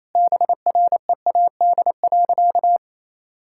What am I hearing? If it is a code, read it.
Morse code, heard as a keyed tone, about 28 wpm, 700 Hz. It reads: BREAD.
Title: Sentences from Top 1600 Words 28wpm